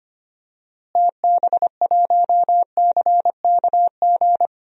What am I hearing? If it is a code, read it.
Morse code, heard as a keyed tone, about 25 wpm, 700 Hz. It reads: TB1CKG
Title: DXCC-100 Call Signs - 6 Characters - Repeated 3x - 25wpm